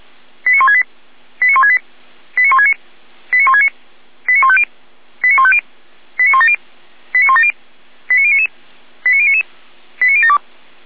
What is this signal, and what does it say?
Five-tone selective call (ZVEI), heard as quick runs of five tones, tones of about 70 ms. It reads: 89168 89168 89169 89169 89150 89150 89170 89170 89099 89099 89072